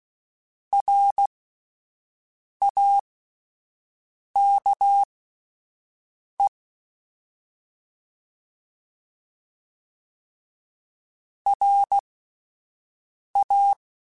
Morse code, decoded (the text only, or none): RAKE RA